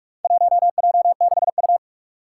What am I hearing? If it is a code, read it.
Morse code, heard as a keyed tone, about 44 wpm, 700 Hz. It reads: 1J6U
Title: Four Challenging Letter-Number Combinations 45wpm